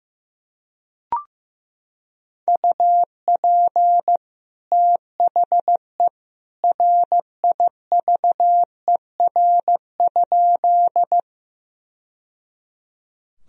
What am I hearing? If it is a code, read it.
Morse code, heard as a keyed tone, about 16 wpm, 700 Hz. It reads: UP THE RIVER?
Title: Sentences from Top 1000 Words - Review Part 1 - 15wpm